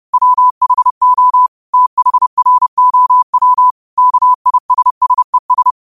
Morse code, heard as a keyed tone, about 30 wpm, 1000 Hz. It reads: WHO THROW KISSES